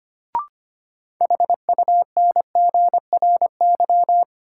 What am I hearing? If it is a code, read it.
Morse code, heard as a keyed tone, about 25 wpm, 700 Hz. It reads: HUNGRY
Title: Sentences from Top 1400 Words - New Word Drill Alt - 25wpm